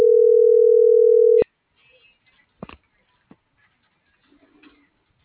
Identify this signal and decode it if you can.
Ringback tone (North America), 440 + 480 Hz.